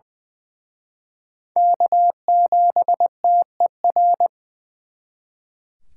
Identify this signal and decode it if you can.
Morse code, heard as a keyed tone, about 24 wpm, 700 Hz. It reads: K7TER
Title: Speed Racing - USA Call Signs 20wpm